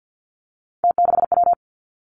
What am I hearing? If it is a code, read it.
Morse code, heard as a keyed tone, about 50 wpm, 700 Hz. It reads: T6W
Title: Three Challenging Letter-Number Combinations 50wpm